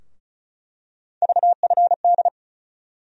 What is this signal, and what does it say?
Morse code, heard as a keyed tone, about 36 wpm, 700 Hz. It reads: VFD
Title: USA Call Sign Postfixes - 3 Letters 35wpm